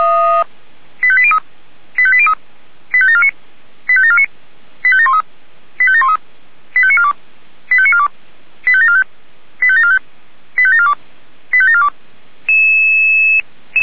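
Five-tone selective call (ZVEI), heard as quick runs of five tones, tones of about 70 ms. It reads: 85803 85803 85749 85749 85713 85713 85942 85942 85745 85745 85742 85742
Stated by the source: ZVEI five-tone call received by SDR.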